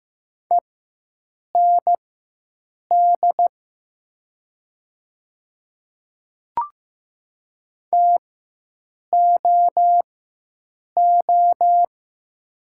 Morse code, heard as a keyed tone, about 15 wpm, spaced as for 7 wpm, 700 Hz. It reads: END TOO
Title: 4x Character Spacing - Top 200 Words - 15wpm